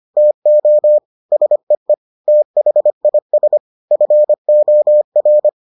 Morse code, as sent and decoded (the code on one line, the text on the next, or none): - --- / ... . / - .... .. ... / ..-. --- .-.
TO SEE THIS FOR